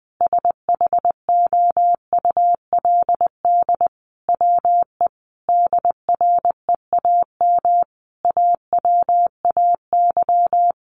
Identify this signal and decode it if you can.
Morse code, heard as a keyed tone, about 20 wpm, 700 Hz. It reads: SHOULD WE DREAM AWAY
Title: Sentences from Top 1500 Words 20wpm